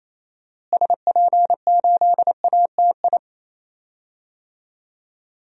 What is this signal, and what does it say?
Morse code, heard as a keyed tone, about 28 wpm, 700 Hz. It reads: SP8ATI
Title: DXCC-100 Call Signs - 6 Characters - Repeated 3x - 28wpm